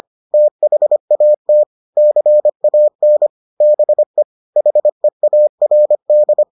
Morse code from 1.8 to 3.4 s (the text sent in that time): CAN